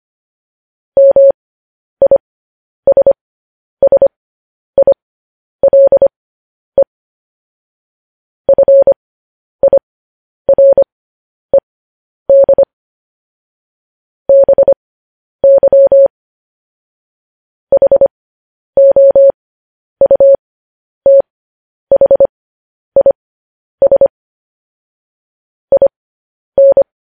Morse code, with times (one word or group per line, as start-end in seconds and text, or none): 1.0-6.8 MISSILE
8.5-12.6 FIRED
14.3-16.1 BY
17.7-24.1 HOUTHIS
25.7-26.8 IN